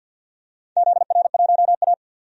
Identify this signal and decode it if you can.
Morse code, heard as a keyed tone, about 53 wpm, 700 Hz. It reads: ZR1A
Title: DXCC-100 Call Signs - 4 Characters - 50wpm